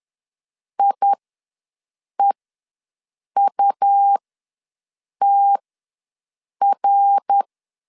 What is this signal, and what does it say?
Morse code, heard as a keyed tone, about 10 wpm, spaced as for 6 wpm, 800 Hz. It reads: IEUTR
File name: LEZIONI / 9 28 ws 10wpm 4124